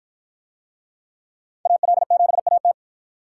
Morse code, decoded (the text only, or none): AL7AT